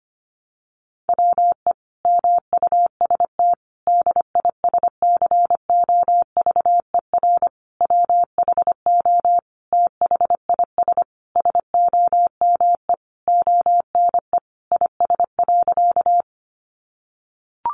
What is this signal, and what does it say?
Morse code, heard as a keyed tone, about 25 wpm, 700 Hz. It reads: WE MUST DISCOVER WHO THIS SOME ONE IS.
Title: Sentences from Top 1100 Words 25wpm